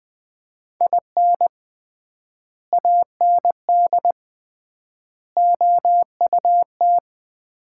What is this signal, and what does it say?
Morse code, heard as a keyed tone, about 20 wpm, 700 Hz. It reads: IN AND OUT
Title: 3x Word-Spacing - Sets of 4 Words - 20wpm